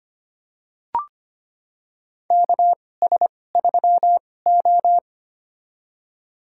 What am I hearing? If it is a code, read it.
Morse code, heard as a keyed tone, about 25 wpm, spaced as for 18 wpm, 700 Hz. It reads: KS3O
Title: USA Call Signs - 2x Character Spacing - 25wpm